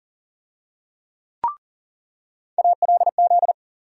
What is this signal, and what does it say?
Morse code, heard as a keyed tone, about 40 wpm, 700 Hz. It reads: ALZ
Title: USA Call Sign Postfixes - 3 Letters 40wpm